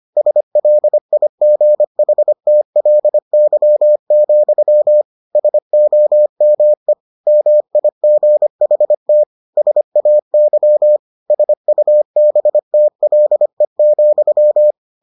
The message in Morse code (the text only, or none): SLIGHTLY, SOME MIGHT SAY SUBTLE,